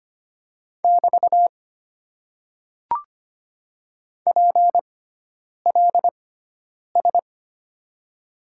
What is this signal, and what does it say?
Morse code, heard as a keyed tone, about 25 wpm, spaced as for 9 wpm, 700 Hz. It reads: <BT> PLS
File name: Top 90 QSO Elements - 6x Character Spacing - 25wpm